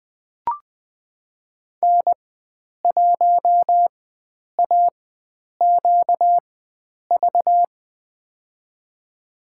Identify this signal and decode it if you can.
Morse code, heard as a keyed tone, about 20 wpm, spaced as for 9 wpm, 700 Hz. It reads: N1AQV
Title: USA Call Signs - 5 Characters - 4x Character Spacing - 20wpm